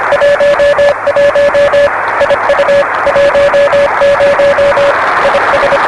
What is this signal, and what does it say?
Morse code, heard as a keyed tone, about 25 wpm, 560 Hz, with noise. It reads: 11 IU 10 IS
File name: RS-7 Beacon 16 Feb 1985 oz6bl